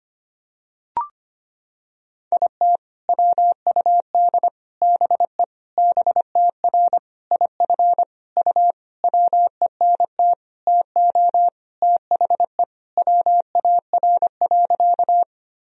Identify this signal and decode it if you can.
Morse code, heard as a keyed tone, about 25 wpm, 700 Hz. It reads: IT WUD BE BTR IF U WENT TO THE WAR.
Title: Sentences from Top 400 Words and Abbreviations 25wpm